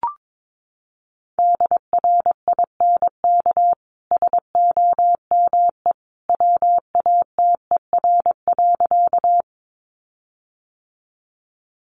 Morse code, heard as a keyed tone, about 22 wpm, 700 Hz. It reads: DRINK SOME WATER.
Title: Sentences from Top 1500 Words 22wpm